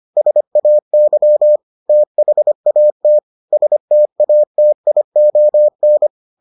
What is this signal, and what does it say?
Morse code, heard as a keyed tone, about 25 wpm, 600 Hz. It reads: SAY THAT STATION